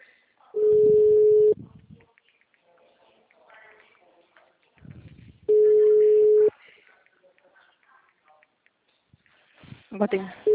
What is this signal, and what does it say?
Ringback tone (Europe, 425 Hz), with speech.
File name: out- -2020- - - .51